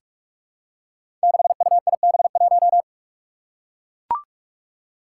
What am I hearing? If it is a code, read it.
Morse code, heard as a keyed tone, about 44 wpm, 700 Hz. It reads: 6UIB1
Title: Five Challenging Letter-Number Combinations 45wpm